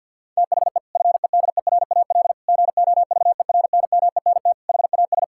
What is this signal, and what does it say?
Morse code, heard as a keyed tone, about 52 wpm, 700 Hz. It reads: THE FEDERAL GOVERNMENT HAS
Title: Sentences from Top 1000 Words - Review Part 5 - 50wpm